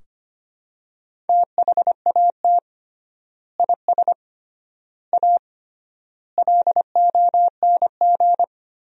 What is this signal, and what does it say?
Morse code, heard as a keyed tone, about 25 wpm, 700 Hz. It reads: THAT IS A LONG
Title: 3x Word-Spacing - Sentences from Top 100 Words - 25wpm